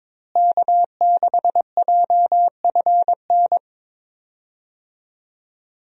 Morse code, heard as a keyed tone, about 22 wpm, 700 Hz. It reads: K6JFN
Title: USA Call Signs - 5 Characters - Repeated 2x - 22wpm